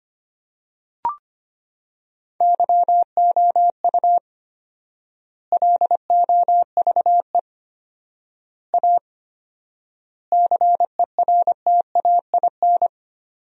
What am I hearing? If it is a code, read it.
Morse code, heard as a keyed tone, about 25 wpm, 700 Hz. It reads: YOU LOVE A CERTAIN
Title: 4x Word-Spacing - Sentences from Top 500 Words - 25wpm